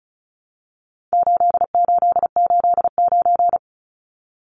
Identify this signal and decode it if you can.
Morse code, heard as a keyed tone, about 34 wpm, 700 Hz. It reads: 8889
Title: Numbers - 4-Digit 35wpm